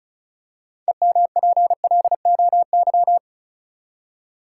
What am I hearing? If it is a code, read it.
Morse code, heard as a keyed tone, about 36 wpm, 700 Hz. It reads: EMPLOY
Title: Six Letter Words - 35wpm